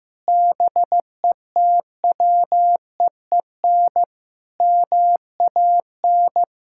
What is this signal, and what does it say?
Morse code, heard as a keyed tone, about 15 wpm, 700 Hz. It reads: BETWEEN MAN
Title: Sentences from Top 1600 Words 15wpm